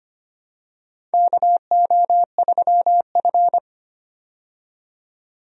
Morse code, heard as a keyed tone, about 25 wpm, 700 Hz. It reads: KO3F
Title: USA Call Signs - 4 Characters - 25wpm